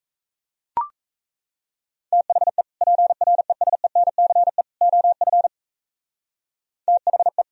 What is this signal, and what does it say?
Morse code, heard as a keyed tone, about 35 wpm, 700 Hz. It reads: THE PRESENCE OF THE